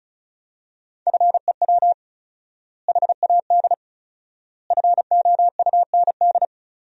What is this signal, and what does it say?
Morse code, heard as a keyed tone, about 36 wpm, 700 Hz. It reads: FEW HAD FOUND
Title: 4x Word-Spacing - Sentences from Top 300 Words - 35wpm